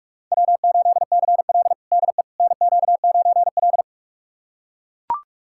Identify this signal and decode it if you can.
Morse code, heard as a keyed tone, about 44 wpm, 690 Hz. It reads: W8CL DE NQ9L
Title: USA Callsign DE Callsign - 4 Characters - 45wpm